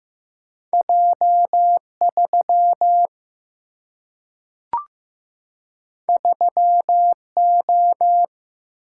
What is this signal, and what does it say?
Morse code, heard as a keyed tone, about 15 wpm, 700 Hz. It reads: J 3 3 O
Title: Single-Letter Single-Number 15wpm